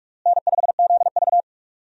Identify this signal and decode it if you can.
Morse code, heard as a keyed tone, about 42 wpm, 700 Hz. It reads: N5ZV
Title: USA Call Signs - 4 Characters - Repeated 3x - 45wpm